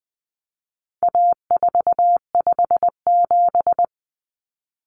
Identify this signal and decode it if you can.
Morse code, heard as a keyed tone, about 20 wpm, 700 Hz. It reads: A457
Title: Common Cut Numbers - 4-Digit 20wpm